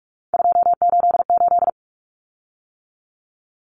Morse code, heard as a keyed tone, about 44 wpm, 700 Hz. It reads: 288
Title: Common Cut Numbers - 3-Digit 45wpm